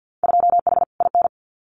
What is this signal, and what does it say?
Morse code, heard as a keyed tone, about 50 wpm, 700 Hz. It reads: 2H IN
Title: Winter Field Day Exchanges Encore 50wpm